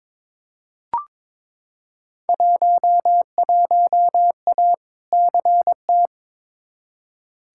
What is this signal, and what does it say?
Morse code, heard as a keyed tone, about 22 wpm, 700 Hz. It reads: 11A CT
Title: Field Day Exchanges Encore - 22wpm